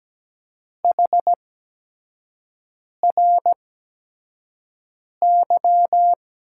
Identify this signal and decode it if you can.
Morse code, heard as a keyed tone, about 17 wpm, spaced as for 5 wpm, 700 Hz. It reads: HRY
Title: USA Call Signs - 6 Characters - 8x Character Spacing - 17wpm